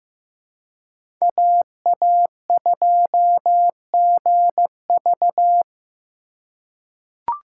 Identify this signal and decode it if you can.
Morse code, heard as a keyed tone, about 15 wpm, 700 Hz. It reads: AA2GV
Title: USA Call Signs - 5 Characters - 3x Character Spacing - 15wpm